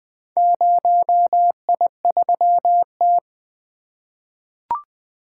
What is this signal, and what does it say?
Morse code, heard as a keyed tone, about 20 wpm, 700 Hz. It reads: 0I3T
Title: Four Letter-Number Combinations - 20wpm